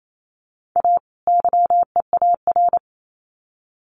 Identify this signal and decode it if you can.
Morse code, heard as a keyed tone, about 28 wpm, 700 Hz. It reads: A YEAR
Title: Sets of 2 Words 28wpm